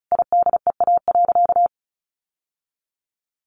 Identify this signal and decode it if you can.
Morse code, heard as a keyed tone, about 34 wpm, 700 Hz. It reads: IDEA.
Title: Sentences from Top 600 Words 35wpm